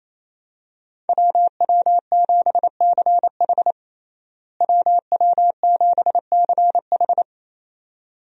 Morse code, heard as a keyed tone, about 28 wpm, 700 Hz. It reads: WW7CH WW7CH
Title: USA Call Signs - 5 Characters - Repeated 2x - 28wpm